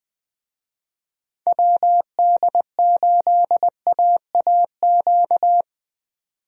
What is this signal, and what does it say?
Morse code, heard as a keyed tone, about 20 wpm, 700 Hz. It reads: WD8AAQ
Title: USA Call Signs - 6 Characters - 20wpm